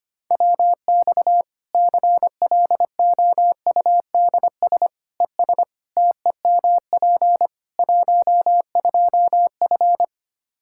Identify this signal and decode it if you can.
Morse code, heard as a keyed tone, about 25 wpm, 700 Hz. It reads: WX CLOUDS ES TEMP 12F
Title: Practice QSO - Exchange 2 - Info - 25wpm